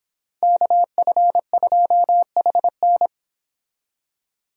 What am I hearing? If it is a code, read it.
Morse code, heard as a keyed tone, about 31 wpm, 700 Hz. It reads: KF2HN